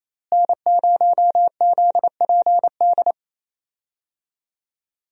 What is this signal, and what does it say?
Morse code, heard as a keyed tone, about 27 wpm, 700 Hz. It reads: N0ZPD
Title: USA Call Signs - 5 Characters - Repeated 3x - 28wpm